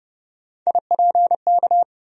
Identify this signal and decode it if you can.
Morse code, heard as a keyed tone, about 29 wpm, 700 Hz. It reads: IPK
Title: USA Call Sign Postfixes - 3 Letters 30wpm